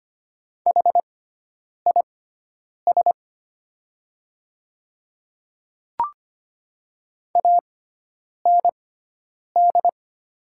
Morse code, heard as a keyed tone, about 25 wpm, spaced as for 9 wpm, 700 Hz. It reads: HIS AND
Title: 6x Character Spacing - Top 200 Words - 25wpm